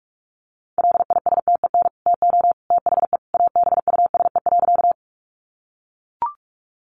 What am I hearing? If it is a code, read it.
Morse code, heard as a keyed tone, about 44 wpm, 700 Hz. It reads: LISTEN TO THE ABUSE.
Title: Sentences from Top 1300 Words 45wpm